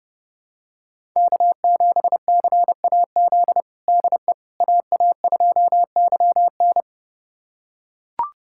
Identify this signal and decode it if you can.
Morse code, heard as a keyed tone, about 30 wpm, 700 Hz. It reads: K7CAZ DE AA2YN
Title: USA Callsign DE Callsign - 5 Characters - 30wpm